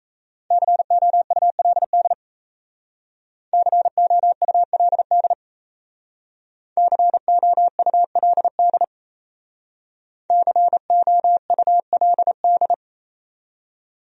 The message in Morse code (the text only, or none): COULD COULD COULD COULD